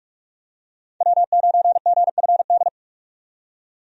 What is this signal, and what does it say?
Morse code, heard as a keyed tone, about 46 wpm, 700 Hz. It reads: W9GFD
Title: USA Call Signs - 5 Characters - 45wpm